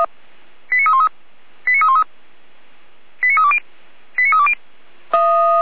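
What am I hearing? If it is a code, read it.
Five-tone selective call (ZVEI), heard as quick runs of five tones, tones of about 70 ms. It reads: 89313 89313 89320 89320
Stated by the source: ZVEI five-tone selective call received on SDR.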